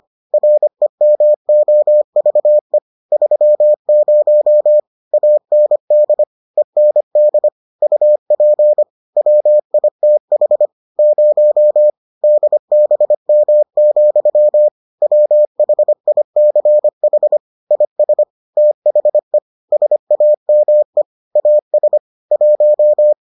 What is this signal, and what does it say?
Morse code, heard as a keyed tone, about 25 wpm, 600 Hz. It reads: REMOVE 30 AND END UP WITH 0 DBM, WHICH IS THE SAME AS 1